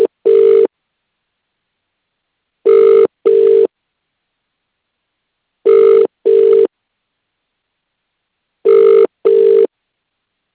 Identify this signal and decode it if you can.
Ringback tone (UK), 400 + 450 Hz.